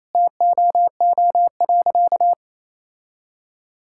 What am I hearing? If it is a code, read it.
Morse code, heard as a keyed tone, about 28 wpm, 700 Hz. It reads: TOO.
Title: Sentences from Top 1000 Words - Review Part 6 - 28wpm